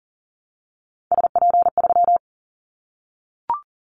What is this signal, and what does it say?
Morse code, heard as a keyed tone, about 40 wpm, 700 Hz. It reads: SP3
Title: Callsign Prefixes for Top 25 DXCC - 40wpm